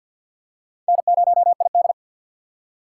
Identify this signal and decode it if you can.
Morse code, heard as a keyed tone, about 50 wpm, 700 Hz. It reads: N0ID
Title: USA Call Signs - 4 Characters - 3x Character Spacing - 50wpm